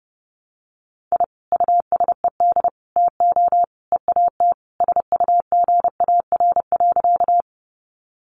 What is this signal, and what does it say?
Morse code, heard as a keyed tone, about 30 wpm, 700 Hz. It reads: I USED TO EAT SUGAR.
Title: Sentences from Top 1200 Words 30wpm